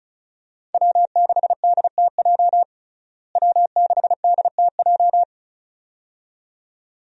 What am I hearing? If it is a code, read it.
Morse code, heard as a keyed tone, about 34 wpm, 700 Hz. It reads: W6DTJ W6DTJ
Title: USA Call Signs - 5 Characters - Repeated 2x - 35wpm